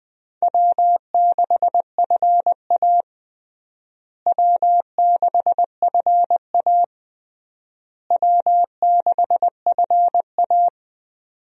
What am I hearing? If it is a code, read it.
Morse code, heard as a keyed tone, about 21 wpm, 700 Hz. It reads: W6FA W6FA W6FA
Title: USA Call Signs - 4 Characters - Repeated 3x - 20wpm